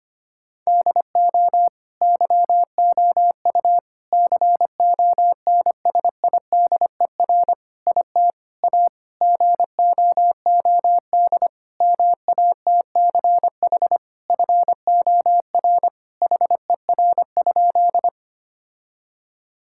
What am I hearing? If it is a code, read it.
Morse code, heard as a keyed tone, about 25 wpm, 700 Hz. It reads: DO YOU CONSIDER IT A GOOD MATCH FOR HER?
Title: Sentences from Top 1600 Words 25wpm